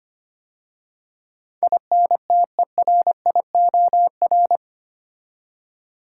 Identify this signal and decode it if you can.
Morse code, heard as a keyed tone, about 25 wpm, 700 Hz. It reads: INTERIOR
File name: 8 Letter Words - First 3 Letters Given - 25wpm